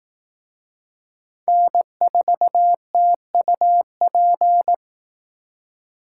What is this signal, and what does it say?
Morse code, heard as a keyed tone, about 18 wpm, 700 Hz. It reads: N4TUP
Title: Speed Racing - USA 1x3 Call Signs 15wpm